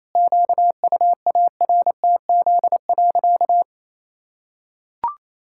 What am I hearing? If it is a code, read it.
Morse code, heard as a keyed tone, about 28 wpm, 700 Hz. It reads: QUARTZ.
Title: Pangram Challenge v2 - 28wpm